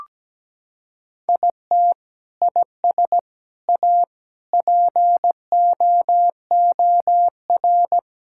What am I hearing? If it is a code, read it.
Morse code, heard as a keyed tone, about 17 wpm, 700 Hz. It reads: IT IS A POOR
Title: Sentences from Top 1000 Words - Review Part 5 - 17wpm